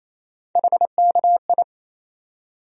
Morse code, heard as a keyed tone, about 28 wpm, 700 Hz. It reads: HKI
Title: USA Call Sign Postfixes - 3 Letters 28wpm